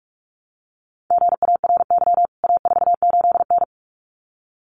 Morse code, heard as a keyed tone, about 44 wpm, 700 Hz. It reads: GARY A48N
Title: CWT Exchange 45wpm